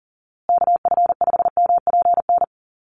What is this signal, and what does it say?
Morse code, heard as a keyed tone, about 40 wpm, 700 Hz. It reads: KF5MPN
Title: USA Call Signs Repeated 2x 40wpm